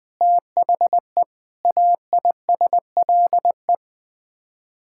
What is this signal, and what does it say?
Morse code, heard as a keyed tone, about 20 wpm, 700 Hz. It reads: THE AISLE
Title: Sets of 6 Words - 20wpm